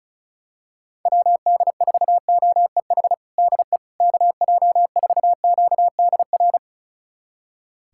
Morse code, heard as a keyed tone, about 36 wpm, 700 Hz. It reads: WD4OEH DE KJ4QDR
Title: USA Callsign DE Callsign - 6 Characters - 35wpm